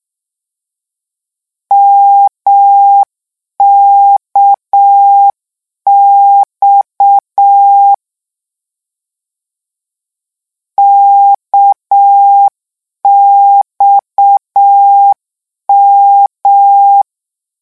Morse code, heard as a keyed tone, about 6 wpm, 800 Hz. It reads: MKX KXM
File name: morse mp3 / lecon 1-vitesse 6